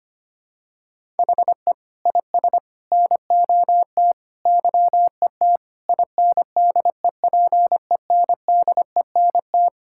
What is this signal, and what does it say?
Morse code, heard as a keyed tone, about 25 wpm, 700 Hz. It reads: HE IS NOT YET INDEPENDENT
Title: Sentences from Top 1000 Words - Review Part 6 - 25wpm